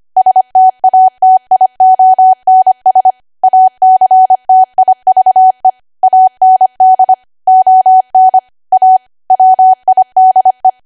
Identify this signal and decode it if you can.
Morse code, heard as a keyed tone, about 25 wpm, 750 Hz. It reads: STATIONS ACTIVE AND ON A WIDE
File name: cwa / W1AW-Level3 / 25WPM